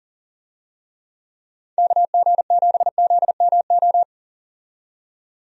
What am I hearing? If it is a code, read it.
Morse code, heard as a keyed tone, about 40 wpm, 700 Hz. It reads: KG7ZMO
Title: USA Call Signs - 6 Characters - 40wpm